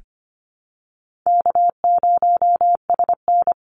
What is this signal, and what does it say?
Morse code, heard as a keyed tone, about 25 wpm, 700 Hz. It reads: K0SN
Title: Call Signs Repeated 2x - 25wpm